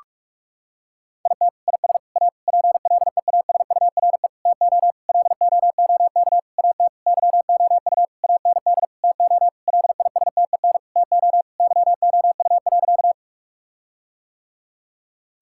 Morse code, heard as a keyed tone, about 46 wpm, 700 Hz. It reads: IT IS A PLEASURE TO LOOK AT YOU AND TO LISTEN TO YOU.
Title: Sentences from Top 1600 Words 45wpm